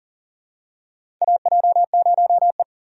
Morse code, heard as a keyed tone, about 40 wpm, 690 Hz. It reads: AJ0E
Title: USA Call Signs - 4 Characters - 2x Character Spacing - 40wpm